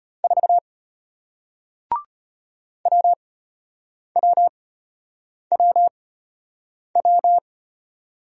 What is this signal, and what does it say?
Morse code, heard as a keyed tone, about 33 wpm, spaced as for 8 wpm, 700 Hz. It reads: 4 WWWW